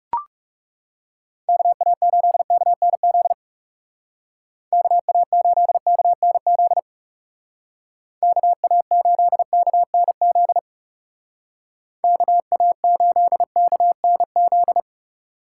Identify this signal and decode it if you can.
Morse code, heard as a keyed tone, about 37 wpm, 700 Hz. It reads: KA8KNZ KA8KNZ KA8KNZ KA8KNZ
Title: Speed Racing - USA 2x3 Call Signs 30wpm